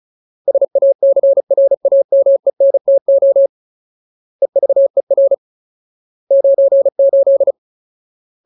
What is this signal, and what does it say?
Morse code, heard as a keyed tone, about 34 wpm, 550 Hz. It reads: SACRAMENTO EVER 98
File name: Ultimate Word List / 35WPM Ultimate